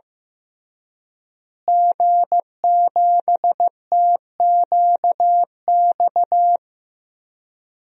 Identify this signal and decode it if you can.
Morse code, heard as a keyed tone, about 14 wpm, 700 Hz. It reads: G7TQX
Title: DXCC-100 Call Signs - 5 Characters - Repeated 3x - 15wpm